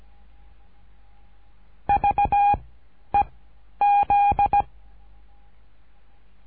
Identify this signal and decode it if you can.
Morse code, heard as a keyed tone, about 16 wpm, 810 Hz. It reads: V E Z
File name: cursos / CURSO DE TELEGRAFIA / Telegrafia - Trilha 13